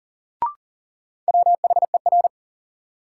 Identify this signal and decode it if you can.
Morse code, heard as a keyed tone, about 40 wpm, 700 Hz. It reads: WHER
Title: Sentences from Top 1500 Words - New Word Drill Alt - 40wpm